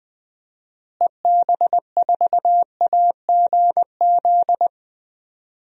Morse code, heard as a keyed tone, about 20 wpm, 700 Hz. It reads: EB4AGZ
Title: DXCC-100 Call Signs - 6 Characters - Repeated 3x - 20wpm